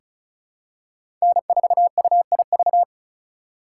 Morse code, heard as a keyed tone, about 36 wpm, 700 Hz. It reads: N4UIV
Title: USA Call Signs - 5 Characters - 4x Character Spacing - 35wpm